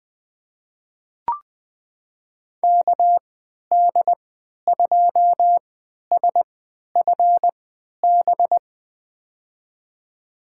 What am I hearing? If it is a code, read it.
Morse code, heard as a keyed tone, about 20 wpm, spaced as for 11 wpm, 700 Hz. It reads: KD2SFB
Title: USA Call Signs - 3x Character Spacing - 20wpm